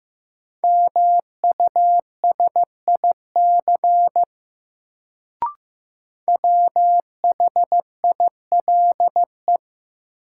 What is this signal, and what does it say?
Morse code, heard as a keyed tone, about 15 wpm, 700 Hz. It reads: MUSIC WHILE
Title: Sentences from Top 300 Words - New Word Drill - 15wpm